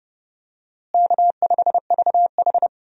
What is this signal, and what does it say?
Morse code, heard as a keyed tone, about 32 wpm, 700 Hz. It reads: K5VH
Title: USA Call Signs - 4 Characters - Repeated 3x - 30wpm